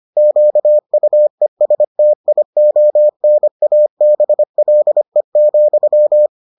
Morse code, heard as a keyed tone, about 25 wpm, 600 Hz. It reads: QUESTIONABLE,